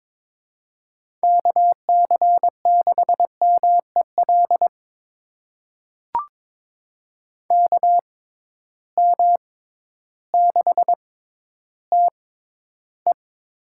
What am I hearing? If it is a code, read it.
Morse code, heard as a keyed tone, about 22 wpm, spaced as for 12 wpm, 700 Hz. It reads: KC6MEL KM6TE
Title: USA Call Signs - 6 Characters - 6x Character Spacing - 22wpm